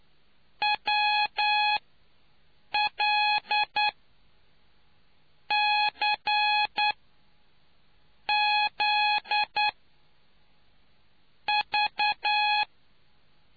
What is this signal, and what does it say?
Morse code, heard as a keyed tone, about 9 wpm, spaced as for 5 wpm, 840 Hz. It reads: WLCZV